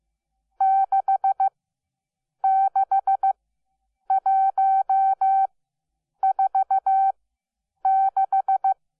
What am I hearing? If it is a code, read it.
Morse code, heard as a keyed tone, about 15 wpm, spaced as for 8 wpm, 770 Hz. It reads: 66146
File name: Onlinelehrgang / morsen / DJ4UF Morsen12